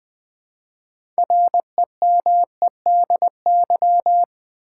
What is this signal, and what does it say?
Morse code, heard as a keyed tone, about 20 wpm, 700 Hz. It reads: REMEDY
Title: 6 Letter Words - First 3 Letters Given - 20wpm